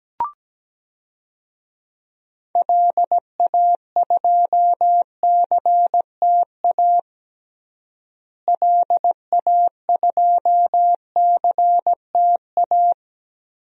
Morse code, heard as a keyed tone, about 17 wpm, 700 Hz. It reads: LA2CTA LA2CTA
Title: DXCC-100 Call Signs - 6 Characters - Repeated 2x - 17wpm